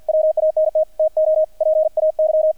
Morse code, heard as a keyed tone, about 50 wpm, 630 Hz, with radio noise. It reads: WANT TO PAY
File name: archived / CRSnet